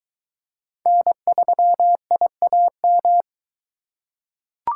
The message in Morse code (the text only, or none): N3IAM